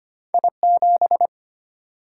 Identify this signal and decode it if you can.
Morse code, heard as a keyed tone, about 25 wpm, 700 Hz. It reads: I7
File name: Two Letter-Number Combinations - 25wpm